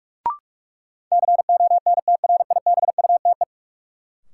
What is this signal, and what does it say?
Morse code, heard as a keyed tone, about 46 wpm, 700 Hz. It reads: CONTRIBUTE
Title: Sentences from Top 1400 Words - New Word Drill Alt - 45wpm